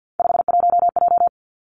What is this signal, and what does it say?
Morse code, heard as a keyed tone, about 50 wpm, 700 Hz. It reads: 51J
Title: Three Challenging Letter-Number Combinations 50wpm